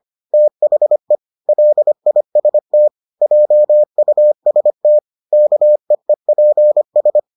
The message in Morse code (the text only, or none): THE LIST JUST KEEPS